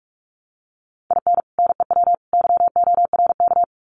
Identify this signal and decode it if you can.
Morse code, heard as a keyed tone, about 44 wpm, 700 Hz. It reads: IN NEW YORK
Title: Sets of 3 Words 45wpm